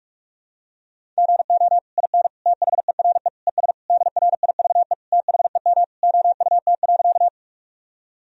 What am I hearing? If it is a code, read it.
Morse code, heard as a keyed tone, about 41 wpm, 700 Hz. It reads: GO IN THERE ES DRIVE THEM OUT.
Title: Sentences from Top 500 Words and Abbreviations 45wpm